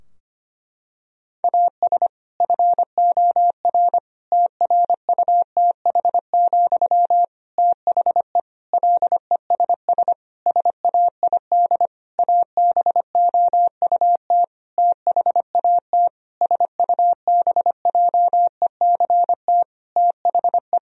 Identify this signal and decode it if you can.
Morse code, heard as a keyed tone, about 25 wpm, 700 Hz. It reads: AS FOR TRUTH, THE LESS SAID ABOUT THAT SUBJECT THE
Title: Sentences from Top 1000 Words - Review Part 2 - 25wpm